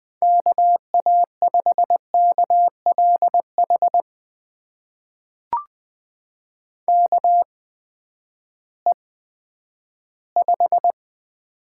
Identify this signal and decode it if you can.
Morse code, heard as a keyed tone, about 20 wpm, spaced as for 11 wpm, 700 Hz. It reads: KA5KLH KE5
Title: USA Call Signs - 8x Character Spacing - 20wpm